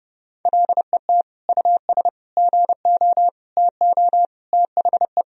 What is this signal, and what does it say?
Morse code, heard as a keyed tone, about 30 wpm, 700 Hz. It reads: LET US GO TO THE
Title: Sentences from Top 1000 Words - Review Part 1 - 30wpm